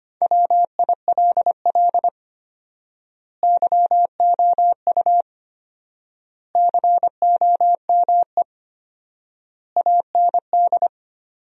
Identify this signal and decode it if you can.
Morse code, heard as a keyed tone, about 25 wpm, 700 Hz. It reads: WILL YOU COME AND
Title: 4x Word-Spacing - Sentences from Top 100 Words - 25wpm